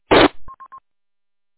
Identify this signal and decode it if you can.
There is no code.